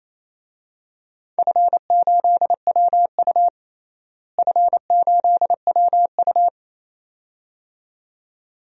Morse code, heard as a keyed tone, about 28 wpm, 700 Hz. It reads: F8WU F8WU
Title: DXCC-100 Call Signs - 4 Characters - Repeated 2x - 28wpm